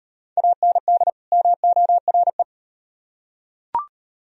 Morse code, heard as a keyed tone, about 36 wpm, 700 Hz. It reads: AND MORE